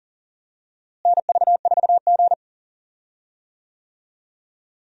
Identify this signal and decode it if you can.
Morse code, heard as a keyed tone, about 40 wpm, 700 Hz. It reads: NV4G
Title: USA Call Signs - 4 Characters - Repeated 3x - 40wpm